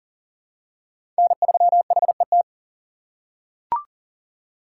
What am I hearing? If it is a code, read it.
Morse code, heard as a keyed tone, about 40 wpm, 700 Hz. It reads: N3HET
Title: USA Call Signs - 5 Characters - 4x Character Spacing - 40wpm